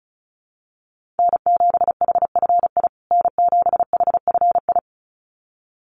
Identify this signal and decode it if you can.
Morse code, heard as a keyed tone, about 34 wpm, 700 Hz. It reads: N7HFI N7HFI
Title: USA Call Signs Repeated 2x 35wpm